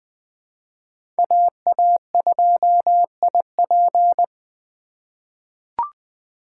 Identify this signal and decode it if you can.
Morse code, heard as a keyed tone, about 20 wpm, 700 Hz. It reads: AA2IP
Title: USA Call Signs - 5 Characters - 4x Character Spacing - 20wpm